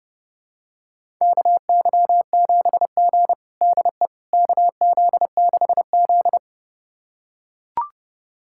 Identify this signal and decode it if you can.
Morse code, heard as a keyed tone, about 32 wpm, 700 Hz. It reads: KY7G DE KZ6Z
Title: USA Callsign DE Callsign - 4 Characters - 30wpm